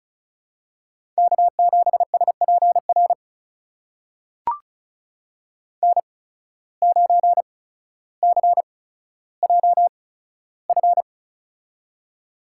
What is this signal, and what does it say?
Morse code, heard as a keyed tone, about 36 wpm, spaced as for 15 wpm, 700 Hz. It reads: K7SPR N9CJF
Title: USA Call Signs - 5 Characters - 8x Character Spacing - 35wpm